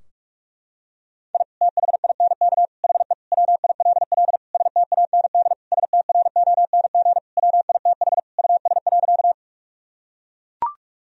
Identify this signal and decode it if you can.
Morse code, heard as a keyed tone, about 41 wpm, 700 Hz. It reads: I THINK HE WILL STAND STRONG WITH US.
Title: Sentences from Top 1000 Words - Review Part 3 - 45wpm